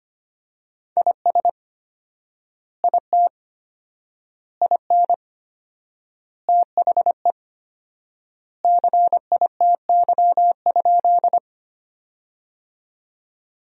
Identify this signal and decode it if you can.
Morse code, heard as a keyed tone, about 25 wpm, 700 Hz. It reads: IS IT IN THE CITY?
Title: 4x Word-Spacing - Sentences from Top 300 Words - 25wpm